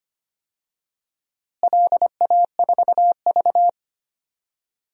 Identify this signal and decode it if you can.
Morse code, heard as a keyed tone, about 25 wpm, 700 Hz. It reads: LA4V
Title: DXCC-100 Call Signs - 4 Characters - Repeated 3x - 25wpm